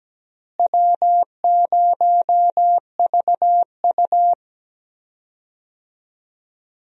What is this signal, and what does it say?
Morse code, heard as a keyed tone, about 17 wpm, 700 Hz. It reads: W0VU
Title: USA Call Signs - 4 Characters - Repeated 2x - 17wpm